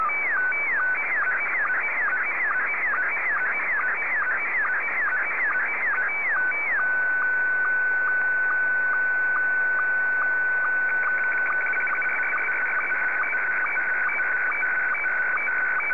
SSTV (slow-scan television), Scottie 1.